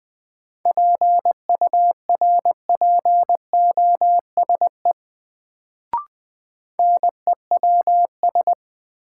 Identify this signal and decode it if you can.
Morse code, heard as a keyed tone, about 20 wpm, 700 Hz. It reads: PURPOSE NEWS
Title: Sets of 5 Words - New Word Drill - 20wpm